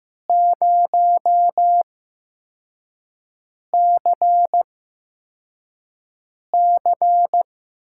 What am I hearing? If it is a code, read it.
Morse code, heard as a keyed tone, about 15 wpm, spaced as for 4 wpm, 700 Hz. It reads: 0CC
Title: USA Call Signs - 5 Characters - 8x Character Spacing - 15wpm